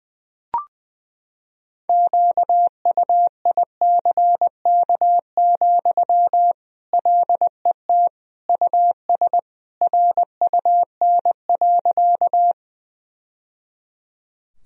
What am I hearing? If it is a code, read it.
Morse code, heard as a keyed tone, about 20 wpm, 700 Hz. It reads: QUICK, LET US RUN.
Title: Sentences from Top 1000 Words - Review Part 1 - 20wpm